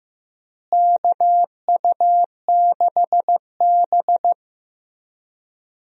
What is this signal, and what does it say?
Morse code, heard as a keyed tone, about 16 wpm, 700 Hz. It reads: KU6B
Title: USA Call Signs - 4 Characters - Repeated 3x - 15wpm